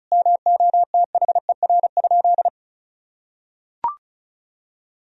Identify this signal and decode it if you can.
Morse code, heard as a keyed tone, about 34 wpm, 700 Hz. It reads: MOTHER?
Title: Sentences from Top 1000 Words - Review Part 2 - 35wpm